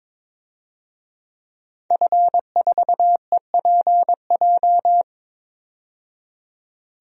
Morse code, heard as a keyed tone, about 21 wpm, 700 Hz. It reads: F4EPJ
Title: DXCC-100 Call Signs - 5 Characters - 22wpm